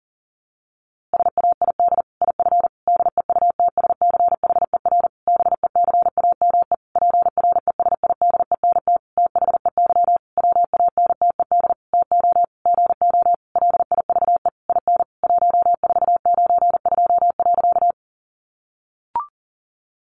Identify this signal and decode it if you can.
Morse code, heard as a keyed tone, about 40 wpm, 700 Hz. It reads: SAID IF DEUTSCHER BECAME PRESIDENT THEY WANTED TO GO LIVE IN 1492.
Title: A Sound Of Thunder 40wpm